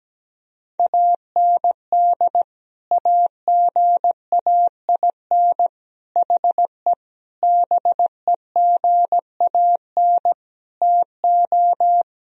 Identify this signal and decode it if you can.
Morse code, heard as a keyed tone, about 17 wpm, 700 Hz. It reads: AND AGAIN HE BEGAN TO
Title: Sentences from Top 1000 Words - Review Part 2 - 17wpm